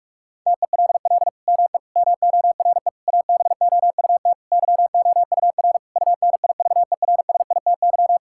Morse code, heard as a keyed tone, about 46 wpm, 700 Hz. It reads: TELL ME MORE ABOUT YOUR UNIVERSITY
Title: Sentences from Top 1000 Words - Review Part 5 - 45wpm